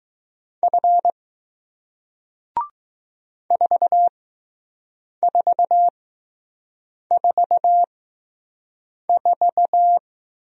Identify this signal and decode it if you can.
Morse code, heard as a keyed tone, about 20 wpm, spaced as for 6 wpm, 700 Hz. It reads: F 4444